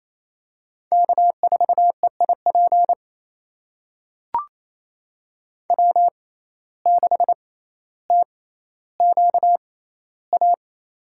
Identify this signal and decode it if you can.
Morse code, heard as a keyed tone, about 27 wpm, spaced as for 14 wpm, 700 Hz. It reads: K4EIP W6TQA
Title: USA Call Signs - 5 Characters - 6x Character Spacing - 28wpm